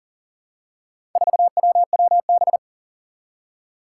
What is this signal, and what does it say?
Morse code, heard as a keyed tone, about 40 wpm, 700 Hz. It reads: 4WWB
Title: Four Challenging Letter-Number Combinations 40wpm